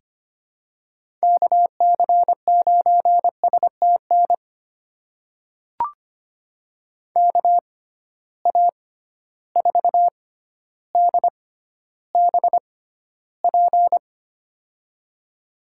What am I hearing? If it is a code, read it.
Morse code, heard as a keyed tone, about 25 wpm, spaced as for 13 wpm, 700 Hz. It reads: KC9STN KA4DBP
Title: USA Call Signs - 6x Character Spacing - 25wpm